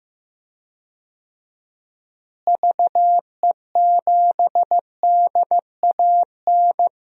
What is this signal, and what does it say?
Morse code, heard as a keyed tone, about 16 wpm, 700 Hz. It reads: VE7DAN
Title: DXCC-100 Call Signs - 6 Characters - Repeated 2x - 15wpm